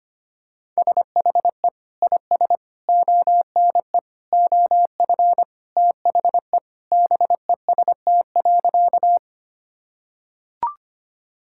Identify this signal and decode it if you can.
Morse code, heard as a keyed tone, about 25 wpm, 690 Hz. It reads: SHE IS ONE OF THE BEST.
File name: practice / 25wpm / Sentences with 2-Word Phrases - 25wpm